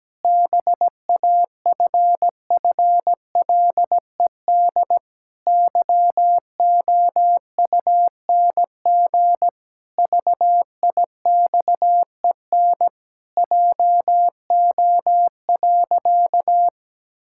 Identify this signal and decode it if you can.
Morse code, heard as a keyed tone, about 17 wpm, 700 Hz. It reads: BAFFLED YOUNG VIXEN JO.
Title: Pangram Challenge v2 - 17wpm